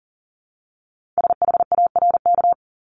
Morse code, heard as a keyed tone, about 38 wpm, 700 Hz. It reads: SHARK
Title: Multi-letter Phonemes Repeated 3x 40wpm